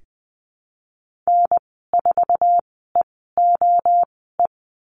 Morse code, heard as a keyed tone, about 20 wpm, spaced as for 15 wpm, 700 Hz. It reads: N4EOE
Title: 147 Lesson 24g - Letter y - Call Signs 20wpm